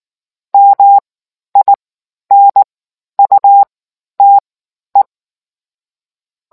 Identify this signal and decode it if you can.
Morse code, heard as a keyed tone, about 19 wpm, spaced as for 11 wpm, 800 Hz. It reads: MINUTE